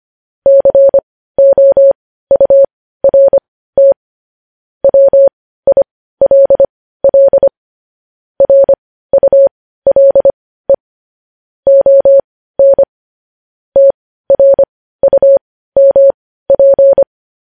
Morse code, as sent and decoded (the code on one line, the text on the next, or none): -.-. --- ..- .-. - / .-- .. .-.. .-.. / .-. ..- .-.. / --- -. / - .-. ..- -- .--.
COURT WILL RULE ON TRUMP